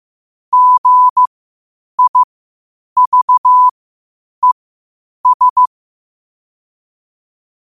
Morse code, heard as a keyed tone, about 15 wpm, spaced as for 8 wpm, 1000 Hz. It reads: GIVES
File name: qotdfiles / 8wpm